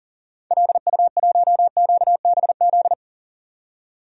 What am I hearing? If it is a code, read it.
Morse code, heard as a keyed tone, about 40 wpm, 700 Hz. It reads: LU1QBZ
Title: DXCC-100 Call Signs - 6 Characters - 40wpm